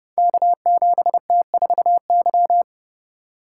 Morse code, heard as a keyed tone, about 30 wpm, 700 Hz. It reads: K7T4Y
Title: Code Groups - 5 Letter:Numbers - 30wpm